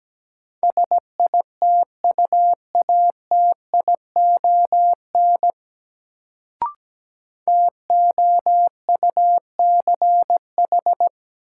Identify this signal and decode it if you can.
Morse code, heard as a keyed tone, about 17 wpm, 700 Hz. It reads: SITUATION TOUCH
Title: Sentences from Top 1200 Words - New Word Drill - 17wpm